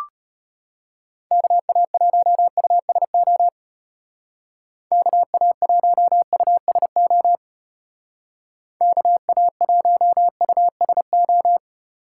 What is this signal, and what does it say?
Morse code, heard as a keyed tone, about 33 wpm, 700 Hz. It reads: KA1USO KA1USO KA1USO